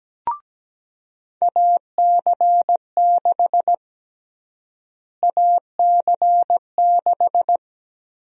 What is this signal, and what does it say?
Morse code, heard as a keyed tone, about 17 wpm, 700 Hz. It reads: AC6 AC6
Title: USA Call Sign Prefixes - Repeated 2x - 17wpm